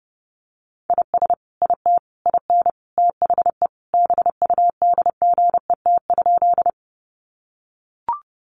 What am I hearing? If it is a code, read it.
Morse code, heard as a keyed tone, about 30 wpm, 700 Hz. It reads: IS IT IN THE BUDGET?
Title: Sentences from Top 1200 Words 30wpm